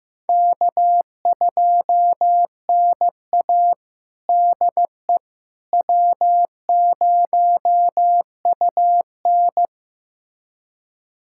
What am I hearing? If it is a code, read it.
Morse code, heard as a keyed tone, about 15 wpm, 700 Hz. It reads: K2NA DE W0UN
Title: USA Callsign DE Callsign - 4 Characters - 15wpm